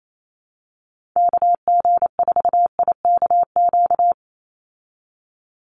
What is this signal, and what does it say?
Morse code, heard as a keyed tone, about 28 wpm, 700 Hz. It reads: KG4IKQ
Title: Call Signs Repeated 2x - 28wpm